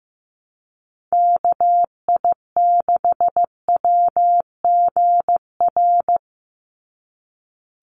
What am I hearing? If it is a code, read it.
Morse code, heard as a keyed tone, about 15 wpm, 700 Hz. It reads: KI6WGR
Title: USA Call Signs 15wpm